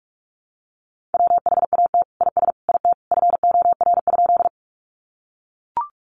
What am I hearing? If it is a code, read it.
Morse code, heard as a keyed tone, about 44 wpm, 700 Hz. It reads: WHAT IS IT FOR?
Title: Sentences from Top 100 Words 45wpm